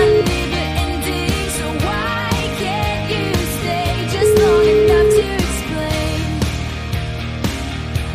Ringback tone (Europe, 425 Hz), with noise.